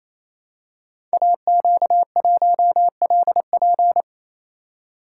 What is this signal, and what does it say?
Morse code, heard as a keyed tone, about 29 wpm, 700 Hz. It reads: AQ1LP
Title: Code Groups - 5 Letter:Numbers - 28wpm